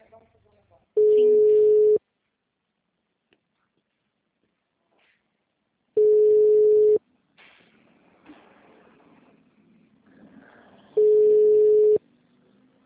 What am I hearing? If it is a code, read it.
Ringback tone (Europe, 425 Hz), with speech.